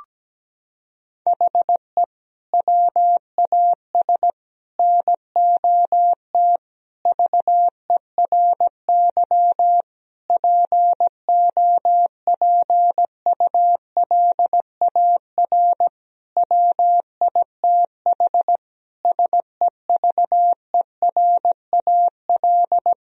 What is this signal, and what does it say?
Morse code, heard as a keyed tone, about 17 wpm, 700 Hz. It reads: HE WAS NOT VERY POPULAR WITH SEVERAL
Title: Sentences from Top 1000 Words - Review Part 3 - 17wpm